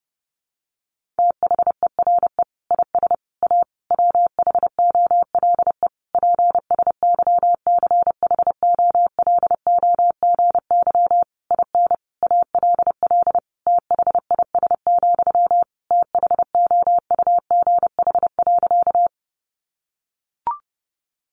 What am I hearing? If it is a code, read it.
Morse code, heard as a keyed tone, about 30 wpm, 700 Hz. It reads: THERE IS A WHOLE PSYCHOLOGY IN ALL THIS, THOUGH.
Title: Sentences from Top 1400 Words 30wpm